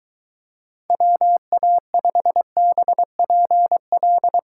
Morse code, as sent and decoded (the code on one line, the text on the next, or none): .-- .- ..... -... .--. .-..
WA5BPL